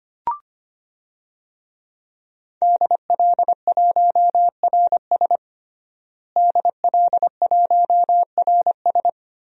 Morse code, heard as a keyed tone, about 25 wpm, 700 Hz. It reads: DL1RS DL1RS
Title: DXCC-100 Call Signs - 5 Characters - Repeated 3x - 25wpm